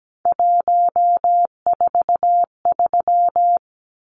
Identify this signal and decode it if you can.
Morse code, heard as a keyed tone, about 17 wpm, 700 Hz. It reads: 143
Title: Numbers - 3-Digit 17wpm